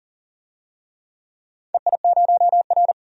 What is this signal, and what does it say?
Morse code, heard as a keyed tone, about 39 wpm, 700 Hz. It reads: EI0R
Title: DXCC-100 Call Signs - 4 Characters - 40wpm